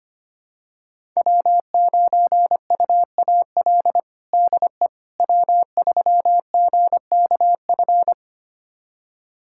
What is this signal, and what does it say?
Morse code, heard as a keyed tone, about 25 wpm, 700 Hz. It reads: W9UAL DE W3GKF
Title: USA Callsign DE Callsign - 5 Characters - 25wpm